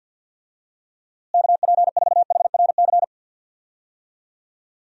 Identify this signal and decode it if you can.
Morse code, heard as a keyed tone, about 53 wpm, 700 Hz. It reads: KP4HRC
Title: USA Call Signs - 6 Characters - Repeated 2x - 50wpm